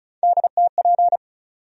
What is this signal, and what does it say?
Morse code, heard as a keyed tone, about 34 wpm, 700 Hz. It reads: DTP
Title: USA Call Sign Postfixes - 3 Letters 35wpm